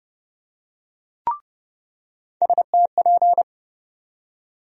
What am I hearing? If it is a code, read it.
Morse code, heard as a keyed tone, about 31 wpm, 700 Hz. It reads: STP
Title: USA Call Sign Postfixes - 3 Letters 30wpm